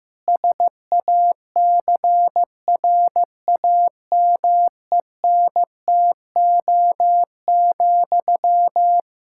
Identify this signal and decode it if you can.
Morse code, heard as a keyed tone, about 16 wpm, 700 Hz. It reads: SACRAMENTO,
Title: K1USN SST QSO Practice 15wpm